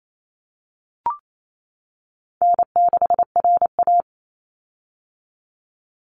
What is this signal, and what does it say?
Morse code, heard as a keyed tone, about 28 wpm, 700 Hz. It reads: N6RA
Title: Easy Call Signs - 28wpm